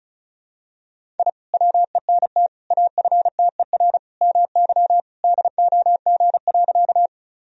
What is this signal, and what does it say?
Morse code, heard as a keyed tone, about 36 wpm, 700 Hz. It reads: I WENT AFTER MY DOG.
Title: Sentences from Top 1000 Words - Review Part 2 - 35wpm